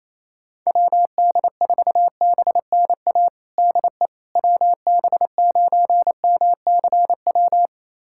Morse code, heard as a keyed tone, about 28 wpm, 700 Hz. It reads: WD4BNA DE WB9MCW
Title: USA Callsign DE Callsign - 6 Characters - 28wpm